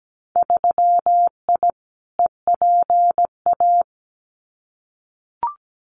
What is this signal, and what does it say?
Morse code, heard as a keyed tone, about 17 wpm, 700 Hz. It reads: 3I EPA
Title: Winter Field Day Exchanges Encore 17wpm